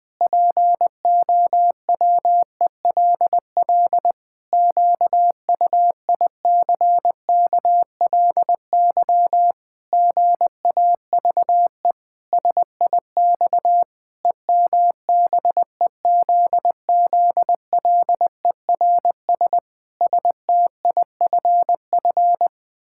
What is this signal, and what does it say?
Morse code, heard as a keyed tone, about 20 wpm, 700 Hz. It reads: POWELL QUICKLY GAVE SIX EMBEZZLERS STIFF
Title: Pangram Challenge v2 - 20wpm